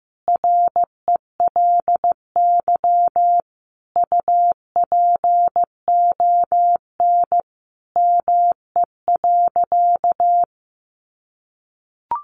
Morse code, heard as a keyed tone, about 15 wpm, 700 Hz. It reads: RELY UPON ME.
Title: Sentences from Top 1200 Words 15wpm